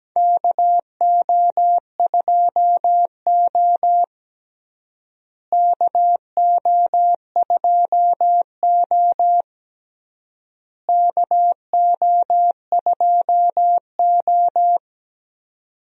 Morse code, heard as a keyed tone, about 17 wpm, 700 Hz. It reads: KO2O KO2O KO2O
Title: USA Call Signs - 4 Characters - Repeated 3x - 17wpm